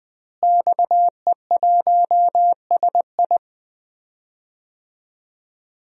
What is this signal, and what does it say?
Morse code, heard as a keyed tone, about 21 wpm, 700 Hz. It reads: XE1SI
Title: DXCC-100 Call Signs - 5 Characters - Repeated 3x - 20wpm